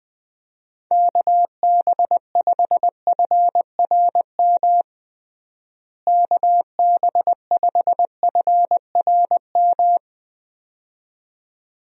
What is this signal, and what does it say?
Morse code, heard as a keyed tone, about 20 wpm, 700 Hz. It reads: KB5FRM KB5FRM
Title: USA Call Signs - 6 Characters - Repeated 2x - 20wpm